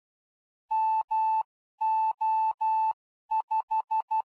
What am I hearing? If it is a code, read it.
Morse code, heard as a keyed tone, about 12 wpm, 880 Hz. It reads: MO5
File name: ardf.quickstart / mo5